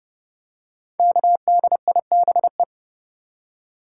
Morse code, heard as a keyed tone, about 30 wpm, 700 Hz. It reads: KDIBE
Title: Code Groups - 5 Letters - 30wpm